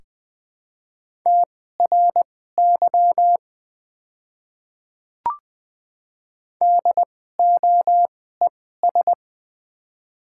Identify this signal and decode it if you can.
Morse code, heard as a keyed tone, about 20 wpm, spaced as for 15 wpm, 700 Hz. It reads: TRY DOES
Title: 2x Character Spacing - Top 200 Words - 20wpm